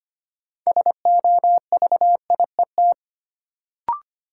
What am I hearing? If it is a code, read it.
Morse code, heard as a keyed tone, about 25 wpm, 700 Hz. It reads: SOVIET
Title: Sets of 5 Words Encore - New Word Drill - 25wpm